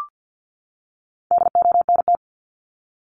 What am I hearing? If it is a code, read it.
Morse code, heard as a keyed tone, about 50 wpm, 700 Hz. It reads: DONT
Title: Top 500 Words 50wpm